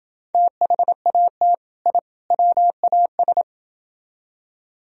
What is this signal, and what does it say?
Morse code, heard as a keyed tone, about 29 wpm, 700 Hz. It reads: THAT I WAS